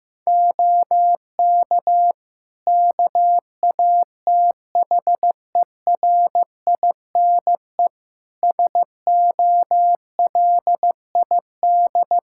Morse code, heard as a keyed tone, about 15 wpm, 700 Hz. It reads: OK KATHERINE SOLID
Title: K1USN SST QSO Practice 15wpm